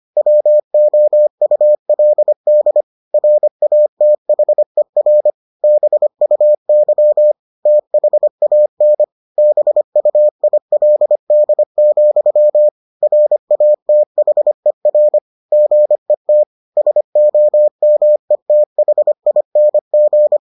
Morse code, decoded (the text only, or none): WOULD RATHER BUY THAN BUILD, RATHER GET SOMETHING